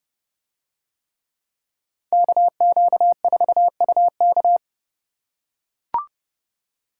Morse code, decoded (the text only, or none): KQ4UK